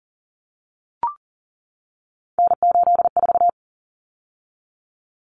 Morse code, heard as a keyed tone, about 40 wpm, 700 Hz. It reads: N84
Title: Common Cut Numbers - 3-Digit 40wpm